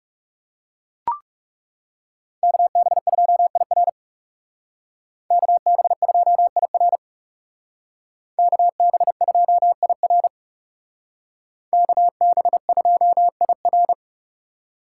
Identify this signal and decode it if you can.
Morse code, heard as a keyed tone, about 36 wpm, 700 Hz. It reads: KB2IR KB2IR KB2IR KB2IR
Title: Speed Racing - USA Call Signs 30wpm